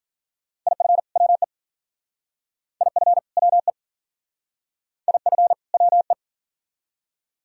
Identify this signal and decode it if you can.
Morse code, heard as a keyed tone, about 47 wpm, 700 Hz. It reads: IF WE IF WE IF WE
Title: Speed-Racing - Sets of 2 Words 40wpm